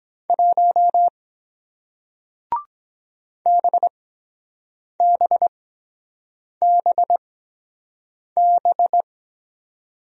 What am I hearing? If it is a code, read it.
Morse code, heard as a keyed tone, about 23 wpm, spaced as for 7 wpm, 700 Hz. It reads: 1 BBBB